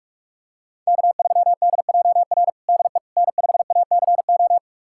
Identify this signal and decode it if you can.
Morse code, heard as a keyed tone, about 46 wpm, 700 Hz. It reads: K3DJR DE N5ACO
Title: USA Callsign DE Callsign - 5 Characters - 45wpm